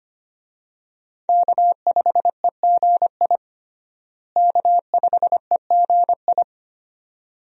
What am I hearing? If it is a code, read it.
Morse code, heard as a keyed tone, about 25 wpm, 700 Hz. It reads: K5EGI K5EGI
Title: USA Call Signs - 5 Characters - Repeated 2x - 25wpm